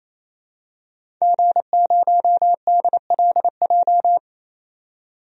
Morse code, heard as a keyed tone, about 28 wpm, 700 Hz. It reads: G0DLJ
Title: DXCC-100 Call Signs - 5 Characters - 28wpm